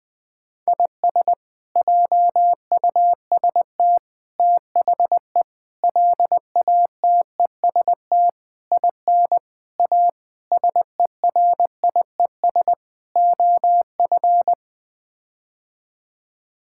Morse code, heard as a keyed tone, about 20 wpm, 700 Hz. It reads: IS JUST THE LATEST IN A SERIES OF
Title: Sets of 8 Words - 20wpm